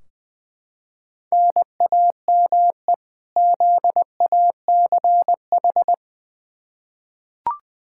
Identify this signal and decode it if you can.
Morse code, heard as a keyed tone, about 21 wpm, 700 Hz. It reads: NAME ZACH
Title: K1USN SST QSO Practice 20wpm